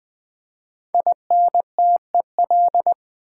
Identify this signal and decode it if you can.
Morse code, heard as a keyed tone, about 20 wpm, 700 Hz. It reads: INTEL
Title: Build A Word - 5 Letters At A Time - 12-Letter Words - 20wpm